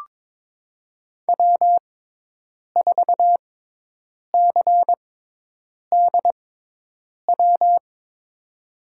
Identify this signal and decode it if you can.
Morse code, heard as a keyed tone, about 22 wpm, spaced as for 8 wpm, 700 Hz. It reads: W4CDW
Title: USA Call Signs - 5 Characters - 6x Character Spacing - 22wpm